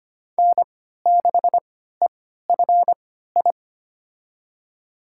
Morse code, heard as a keyed tone, about 25 wpm, spaced as for 14 wpm, 700 Hz. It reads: N6EFI
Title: USA Call Signs - 5 Characters - 3x Character Spacing - 25wpm